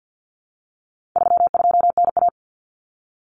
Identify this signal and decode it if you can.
Morse code, heard as a keyed tone, about 50 wpm, 700 Hz. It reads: 32NA
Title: Common Cut Numbers - 4-Digit 50wpm